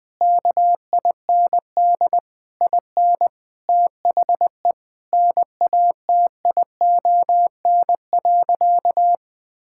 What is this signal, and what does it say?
Morse code, heard as a keyed tone, about 20 wpm, 700 Hz. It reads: KIND IN THE NATION.
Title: Sentences with 8-Word Phrases - Spoken Beginning - 20wpm